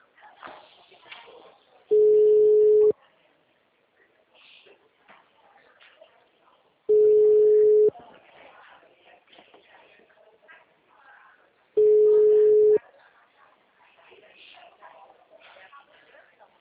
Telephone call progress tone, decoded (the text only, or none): ringback tone